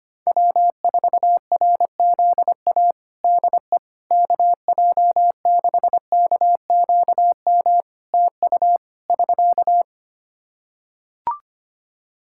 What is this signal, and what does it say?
Morse code, heard as a keyed tone, about 25 wpm, 700 Hz. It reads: W4RZA DE KJ6KQM TU <SK>
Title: Practice QSO - Exchange 4 - Ending - 25wpm